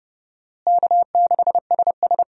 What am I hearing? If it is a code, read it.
Morse code, heard as a keyed tone, about 30 wpm, 700 Hz. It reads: K6SS
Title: USA Call Signs - 4 Characters - Repeated 2x - 30wpm